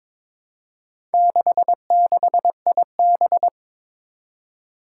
Morse code, heard as a keyed tone, about 22 wpm, 700 Hz. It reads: 66IB
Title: Four Challenging Letter-Number Combinations 22wpm